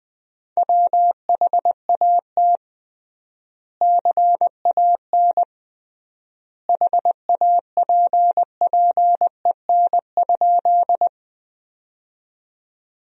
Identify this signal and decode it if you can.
Morse code, heard as a keyed tone, about 21 wpm, 700 Hz. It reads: WHAT CAN HAPPEN?
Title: 3x Word-Spacing - Sentences from Top 400 Words - 20wpm